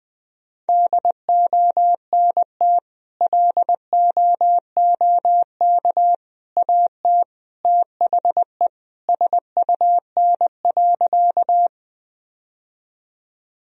Morse code, heard as a keyed tone, about 19 wpm, 700 Hz. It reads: DONT LOOK AT THE SUN.
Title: Sentences from Top 1000 Words - Review Part 1 - 20wpm